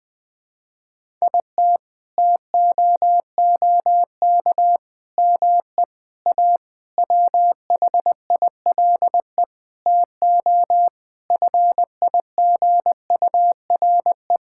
Morse code, heard as a keyed tone, about 20 wpm, 700 Hz. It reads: IT TOOK ME A WHILE TO FIGURE
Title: Sentences with 8-Word Phrases - Spoken Beginning - 20wpm